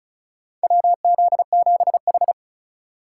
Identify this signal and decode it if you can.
Morse code, heard as a keyed tone, about 36 wpm, 700 Hz. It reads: WZ7H
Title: USA Call Signs - 2x Character Spacing - 35wpm